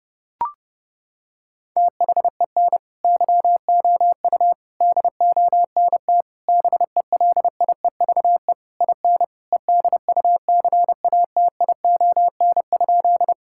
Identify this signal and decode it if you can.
Morse code, heard as a keyed tone, about 30 wpm, 700 Hz. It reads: THEN YOU DONT BELIEVE IN EDUCATION?
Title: Sentences from Top 1000 Words - Review Part 6 - 30wpm